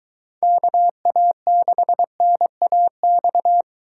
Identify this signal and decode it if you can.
Morse code, heard as a keyed tone, about 22 wpm, 700 Hz. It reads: KA6NAX